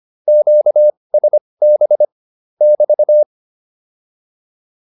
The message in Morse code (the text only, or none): QSB <BT>